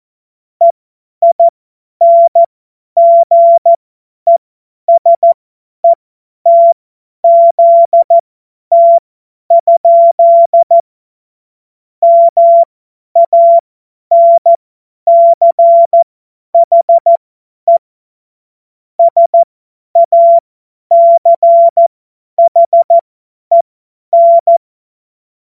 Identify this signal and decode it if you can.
Morse code, heard as a keyed tone, about 14 wpm, spaced as for 10 wpm, 680 Hz. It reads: EINGESETZT? MANCHE SACHEN